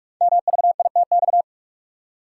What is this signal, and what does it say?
Morse code, heard as a keyed tone, about 44 wpm, 690 Hz. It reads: MVITX
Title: Code Groups - 5 Letters - 45wpm